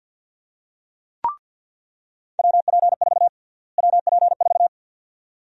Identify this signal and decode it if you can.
Morse code, heard as a keyed tone, about 49 wpm, 700 Hz. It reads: WP4 WP4
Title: USA Call Sign Prefixes - Repeated 2x - 50wpm